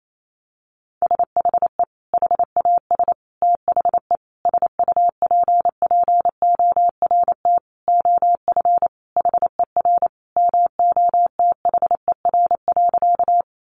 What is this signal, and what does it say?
Morse code, heard as a keyed tone, about 28 wpm, 700 Hz. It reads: SHE HAS THE SUPPORT OF HER MOTHER.
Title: Sentences from Top 800 Words 28wpm